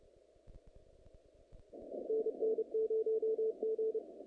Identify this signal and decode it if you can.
Morse code, heard as a keyed tone, about 29 wpm, 440 Hz. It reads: NN0G
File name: contests / naqp-cw-k5tr-op-k5pi- -2200